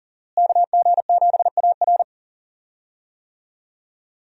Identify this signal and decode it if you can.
Morse code, heard as a keyed tone, about 40 wpm, 700 Hz. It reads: KG7AR
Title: USA Call Signs - 5 Characters - Repeated 3x - 40wpm